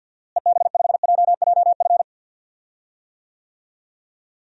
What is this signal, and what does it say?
Morse code, heard as a keyed tone, about 52 wpm, 700 Hz. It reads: EB5JJF
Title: DXCC-100 Call Signs - 6 Characters - Repeated 2x - 50wpm